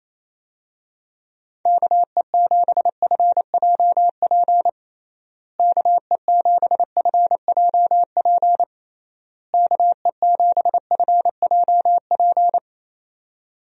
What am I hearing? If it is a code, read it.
Morse code, heard as a keyed tone, about 27 wpm, 700 Hz. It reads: KE7FJP KE7FJP KE7FJP
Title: USA Call Signs - 6 Characters - Repeated 3x - 28wpm